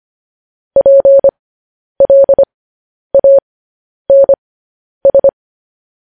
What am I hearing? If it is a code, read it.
Morse code, heard as a keyed tone, about 25 wpm, spaced as for 10 wpm, 550 Hz. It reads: PLANS